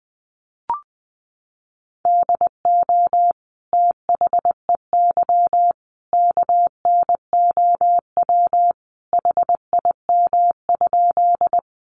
Morse code, heard as a keyed tone, about 20 wpm, 700 Hz. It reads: DO THEY KNOW HIM?
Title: Sentences from Top 100 Words 20wpm